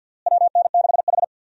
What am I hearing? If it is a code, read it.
Morse code, heard as a keyed tone, about 51 wpm, 700 Hz. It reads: WN6H
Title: USA Call Signs - 4 Characters - Repeated 2x - 50wpm